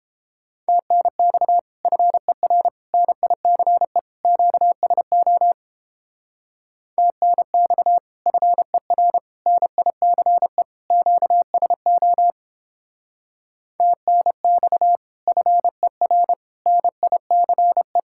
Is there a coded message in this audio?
Yes: Morse code from 0.7 to 18.0 s, heard as a keyed tone, about 31 wpm, 700 Hz. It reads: TNX FER NICE QSO TNX FER NICE QSO TNX FER NICE